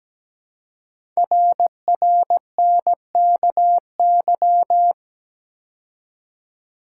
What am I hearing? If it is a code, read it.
Morse code, heard as a keyed tone, about 17 wpm, 700 Hz. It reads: RRNKY
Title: Code Groups - 5 Letters - 17wpm